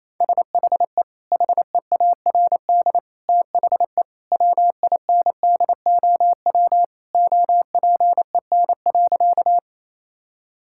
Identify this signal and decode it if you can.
Morse code, heard as a keyed tone, about 28 wpm, 700 Hz. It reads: SHE HEARD THE WINDOW OPEN.
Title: Sentences from Top 1600 Words 28wpm